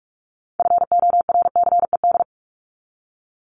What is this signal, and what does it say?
Morse code, heard as a keyed tone, about 44 wpm, 700 Hz. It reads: FORCED
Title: ED Suffix - Top 100 Words with Base Words 45wpm